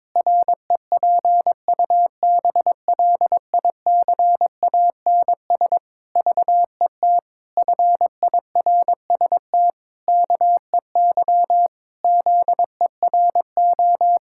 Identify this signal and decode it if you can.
Morse code, heard as a keyed tone, about 22 wpm, 700 Hz. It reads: REPUBLICANS VET FIRST KEY ZERO
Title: Pangram Challenge v2 - 22wpm